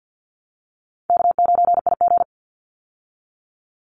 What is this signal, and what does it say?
Morse code, heard as a keyed tone, about 50 wpm, 700 Hz. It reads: K9IG
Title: Easy Call Signs - 50wpm